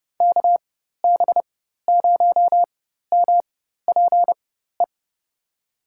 Morse code, heard as a keyed tone, about 32 wpm, spaced as for 14 wpm, 700 Hz. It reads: KB0MPE
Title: USA Call Signs - 6 Characters - 4x Character Spacing - 30wpm